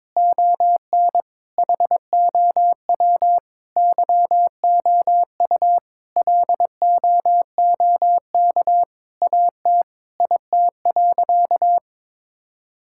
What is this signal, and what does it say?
Morse code, heard as a keyed tone, about 21 wpm, 700 Hz. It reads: ON HOW YOU LOOK AT IT.
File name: Sentences with 7-Word Phrases - 22wpm